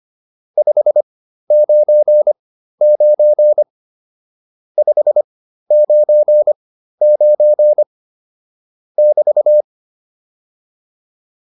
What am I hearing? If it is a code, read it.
Morse code, heard as a keyed tone, about 25 wpm, spaced as for 13 wpm, 600 Hz. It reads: 599 599 <BT>
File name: QSO208 13